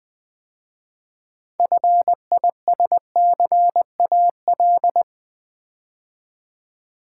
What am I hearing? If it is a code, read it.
Morse code, heard as a keyed tone, about 20 wpm, 700 Hz. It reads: FISCAL
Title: 6 Letter Words - First 3 Letters Given - 20wpm